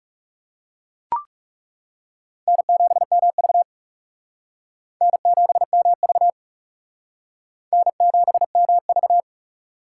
Morse code, heard as a keyed tone, about 37 wpm, 700 Hz. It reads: N7MV N7MV N7MV
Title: Speed Racing - USA Call Signs 30wpm